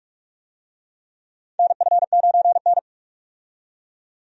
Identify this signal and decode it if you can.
Morse code, heard as a keyed tone, about 43 wpm, 700 Hz. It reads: NF9N
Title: USA Call Signs - 4 Characters - 45wpm